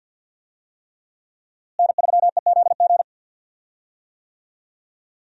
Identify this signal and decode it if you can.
Morse code, heard as a keyed tone, about 47 wpm, 700 Hz. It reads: N3EZG
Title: USA Call Signs - 5 Characters - Repeated 3x - 50wpm